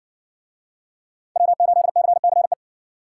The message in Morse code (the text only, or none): W8CCE